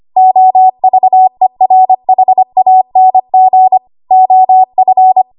Morse code, heard as a keyed tone, about 25 wpm, 750 Hz. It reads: OVERHANG OF
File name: Archive / 25 WPM / 25WPM